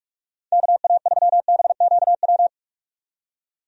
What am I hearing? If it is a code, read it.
Morse code, heard as a keyed tone, about 48 wpm, 700 Hz. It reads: KA3BQW
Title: USA Call Signs - 6 Characters - 45wpm